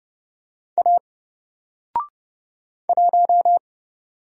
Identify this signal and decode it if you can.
Morse code, heard as a keyed tone, about 30 wpm, 700 Hz. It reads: A 1
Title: Single Letter-Number 30wpm